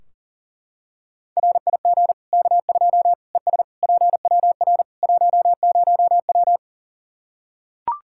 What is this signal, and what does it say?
Morse code, heard as a keyed tone, about 40 wpm, 700 Hz. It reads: RIG K2 ES PWR 10W
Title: Practice QSO - Exchange 2 - Info - 40wpm